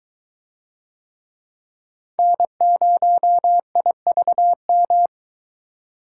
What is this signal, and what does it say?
Morse code, heard as a keyed tone, about 22 wpm, 700 Hz. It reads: N0IVM